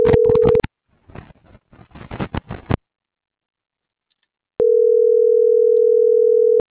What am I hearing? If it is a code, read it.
Ringback tone (North America), 440 + 480 Hz.